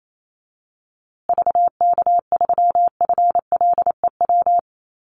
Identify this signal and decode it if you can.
Morse code, heard as a keyed tone, about 28 wpm, 700 Hz. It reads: VK3FLEW
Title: USA Call Signs Repeated 3x 28wpm